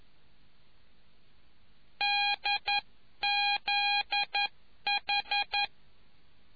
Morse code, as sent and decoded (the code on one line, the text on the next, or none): -.. --.. ....
DZH